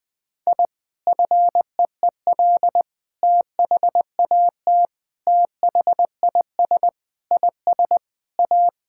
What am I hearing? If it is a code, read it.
Morse code, heard as a keyed tone, about 20 wpm, 700 Hz. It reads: I FEEL THAT THIS IS A
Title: Sentences with 4-Word Phrases - Spoken Beginning - 20wpm